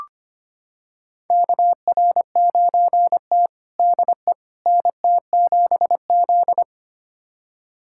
Morse code, heard as a keyed tone, about 25 wpm, 700 Hz. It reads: KR9T DE NT7Z
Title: USA Callsign DE Callsign - 4 Characters - 25wpm